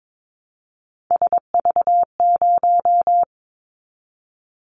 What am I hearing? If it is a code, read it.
Morse code, heard as a keyed tone, about 22 wpm, 700 Hz. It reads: SV0
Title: Callsign Prefixes for Top 50 DXCC - 22wpm